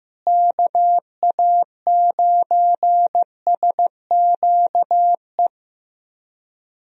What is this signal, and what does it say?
Morse code, heard as a keyed tone, about 15 wpm, 700 Hz. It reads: KA9SQE
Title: USA Call Signs - 6 Characters - Repeated 2x - 15wpm